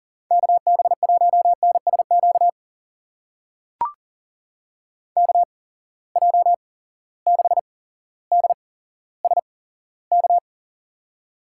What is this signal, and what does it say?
Morse code, heard as a keyed tone, about 37 wpm, spaced as for 17 wpm, 700 Hz. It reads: KB1NSQ KJ6DSK
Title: USA Call Signs - 6 Characters - 8x Character Spacing - 40wpm